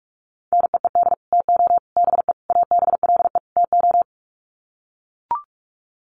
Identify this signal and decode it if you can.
Morse code, heard as a keyed tone, about 44 wpm, 700 Hz. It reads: NEED TO BE ABLE TO
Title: Sets of 5 Words Encore 45wpm